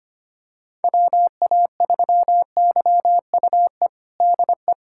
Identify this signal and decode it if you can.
Morse code, heard as a keyed tone, about 25 wpm, 700 Hz. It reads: WA3YUE DE
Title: USA Callsign DE Callsign 25wpm